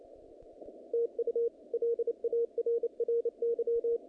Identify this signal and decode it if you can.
Morse code, heard as a keyed tone, about 28 wpm, 470 Hz. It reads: TU LARRY
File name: contests / naqp-cw-k5tr-op-k5pi- -2200